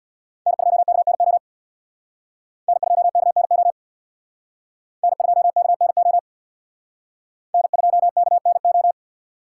Morse code, heard as a keyed tone, about 60 wpm, 700 Hz. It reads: N2KNO N2KNO N2KNO N2KNO
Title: Speed Racing - USA 1x3 Call Signs 50wpm